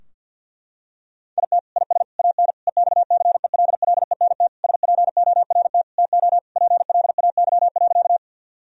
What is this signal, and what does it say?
Morse code, heard as a keyed tone, about 50 wpm, 700 Hz. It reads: IT IS AN EXCELLENT SPORT TO PLAY.
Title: Sentences from Top 1000 Words - Review Part 4 - 50wpm